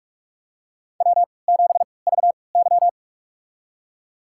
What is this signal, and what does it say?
Morse code, heard as a keyed tone, about 46 wpm, spaced as for 26 wpm, 700 Hz. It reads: W7VY
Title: USA Call Signs - 4 Characters - 3x Character Spacing - 45wpm